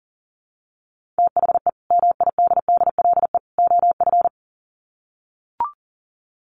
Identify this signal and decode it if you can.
Morse code, heard as a keyed tone, about 40 wpm, 700 Hz. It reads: THE MIDDLE OF
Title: Sets of 3 Words 40wpm